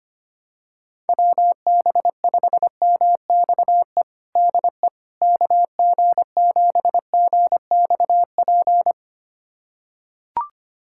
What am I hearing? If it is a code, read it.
Morse code, heard as a keyed tone, about 25 wpm, 700 Hz. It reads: WB5MXE DE KG7GXP
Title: USA Callsign DE Callsign - 6 Characters - 25wpm